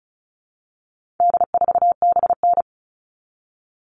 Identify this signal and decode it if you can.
Morse code, heard as a keyed tone, about 34 wpm, 700 Hz. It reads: D4BN
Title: All Cut Numbers - 4-Digit 35wpm